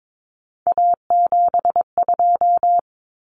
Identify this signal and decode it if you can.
Morse code, heard as a keyed tone, about 22 wpm, 700 Hz. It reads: A72
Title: Common Cut Numbers - 3-Digit 22wpm